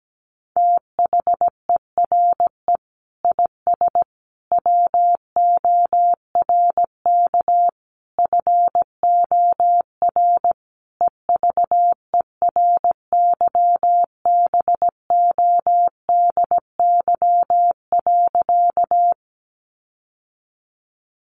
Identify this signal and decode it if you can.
Morse code, heard as a keyed tone, about 17 wpm, 700 Hz. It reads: THERE IS WORK FOR EVERYBODY.
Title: Sentences from Top 1400 Words 17wpm